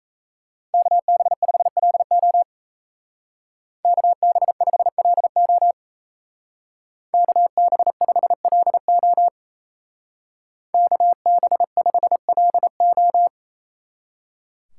Morse code, heard as a keyed tone, about 36 wpm, 700 Hz. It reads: KB5LO KB5LO KB5LO KB5LO